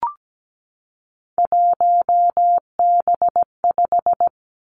Morse code, heard as a keyed tone, about 17 wpm, 700 Hz. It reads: 1B5
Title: Three Challenging Letter-Number Combinations 17wpm